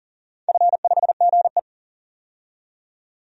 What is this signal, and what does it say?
Morse code, heard as a keyed tone, about 40 wpm, 700 Hz. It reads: F5GE
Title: DXCC-100 Call Signs - 4 Characters - 40wpm